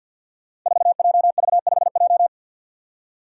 Morse code, heard as a keyed tone, about 53 wpm, 700 Hz. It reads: 4JV5J
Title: Five Challenging Letter-Number Combinations 50wpm